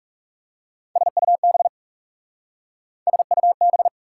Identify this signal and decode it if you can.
Morse code, heard as a keyed tone, about 40 wpm, 690 Hz. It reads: SUB SUB